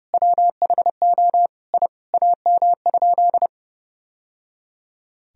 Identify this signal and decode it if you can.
Morse code, heard as a keyed tone, about 30 wpm, 700 Hz. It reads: WHO I AM?
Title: Sentences from Top 1000 Words - Review Part 2 - 30wpm